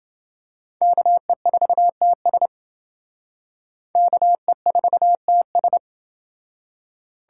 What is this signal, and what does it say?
Morse code, heard as a keyed tone, about 28 wpm, 700 Hz. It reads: KE4TS KE4TS